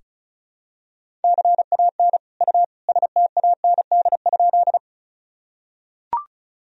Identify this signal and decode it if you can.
Morse code, heard as a keyed tone, about 36 wpm, 700 Hz. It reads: CAN U STAND?
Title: Sentences from Top 300 Words and Abbreviations 35wpm